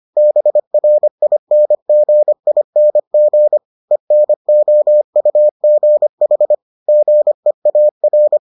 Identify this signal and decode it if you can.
Morse code, heard as a keyed tone, about 25 wpm, 600 Hz. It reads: BRINGING ENOUGH GEAR